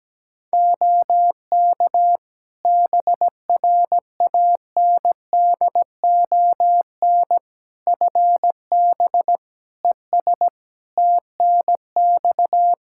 Morse code, heard as a keyed tone, about 17 wpm, 700 Hz. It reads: OK BRANDON FB ES TNX
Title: K1USN SST QSO Practice 17wpm